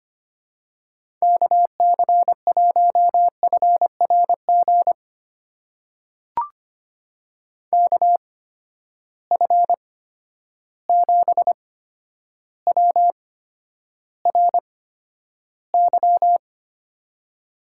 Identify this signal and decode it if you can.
Morse code, heard as a keyed tone, about 25 wpm, spaced as for 11 wpm, 700 Hz. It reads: KC1FRG KF7WRY
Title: USA Call Signs - 6 Characters - 8x Character Spacing - 25wpm